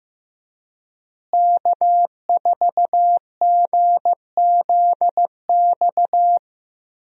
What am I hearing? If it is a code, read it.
Morse code, heard as a keyed tone, about 15 wpm, 700 Hz. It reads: K4GZX
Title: USA Call Signs - 5 Characters - Repeated 2x - 15wpm